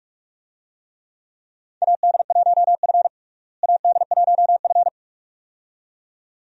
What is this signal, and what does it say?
Morse code, heard as a keyed tone, about 43 wpm, 700 Hz. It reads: AD1F AD1F
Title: USA Call Signs - 4 Characters - Repeated 2x - 45wpm